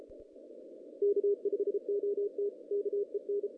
Morse code, heard as a keyed tone, about 32 wpm, 380 Hz. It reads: K5OT KEN
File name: 2016 JAN CW NAQP / 2016 Jan naqp k5ot - 0-11